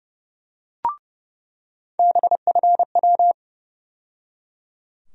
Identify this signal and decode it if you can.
Morse code, heard as a keyed tone, about 30 wpm, 700 Hz. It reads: BFW
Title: USA Call Sign Postfixes - 3 Letters 30wpm